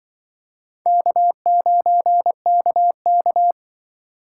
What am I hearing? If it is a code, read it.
Morse code, heard as a keyed tone, about 25 wpm, 700 Hz. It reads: K9KK
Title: Speed Racing - USA 1x2 Call Signs 20wpm